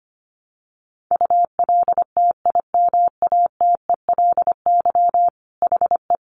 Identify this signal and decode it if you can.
Morse code, heard as a keyed tone, about 25 wpm, 700 Hz. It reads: ULTIMATELY HE
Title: Sentences from Top 1200 Words 25wpm